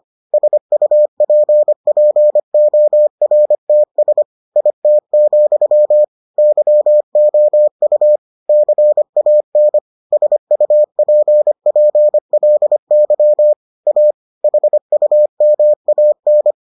Morse code, heard as a keyed tone, about 25 wpm, 600 Hz. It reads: SUPPORTS IT, YOU CAN SUPPLY A HUMAN